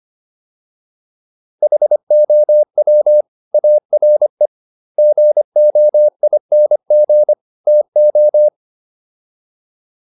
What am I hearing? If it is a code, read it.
Morse code, heard as a keyed tone, about 25 wpm, 600 Hz. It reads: HOW ARE GOING TO